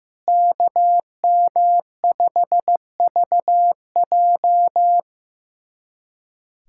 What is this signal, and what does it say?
Morse code, heard as a keyed tone, about 16 wpm, 700 Hz. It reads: KM5VJ
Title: USA Callsign DE Callsign - 5 Characters - 15wpm